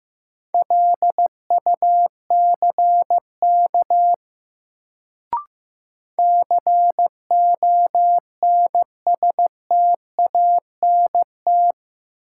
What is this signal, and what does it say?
Morse code, heard as a keyed tone, about 15 wpm, 700 Hz. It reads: LUCK CONSTANT
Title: Sentences from Top 1500 Words - New Word Drill - 15wpm